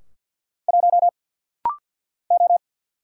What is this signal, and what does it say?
Morse code, heard as a keyed tone, about 50 wpm, 690 Hz. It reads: 1 O
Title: Single Letter-Number - Rapid-Fire 50wpm